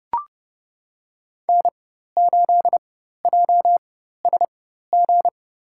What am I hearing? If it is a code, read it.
Morse code, heard as a keyed tone, about 32 wpm, spaced as for 14 wpm, 700 Hz. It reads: N8JSG
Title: USA Call Signs - 5 Characters - 4x Character Spacing - 30wpm